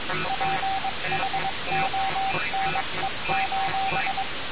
Morse code, heard as a keyed tone, about 22 wpm, 770 Hz, with noise. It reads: PHONE:
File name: radiocap / REC z